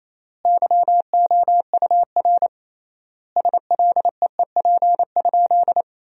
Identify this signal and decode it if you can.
Morse code, heard as a keyed tone, about 28 wpm, 700 Hz. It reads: YOUR SLEEP?
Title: 3x Word-Spacing - Sentences from Top 500 Words - 28wpm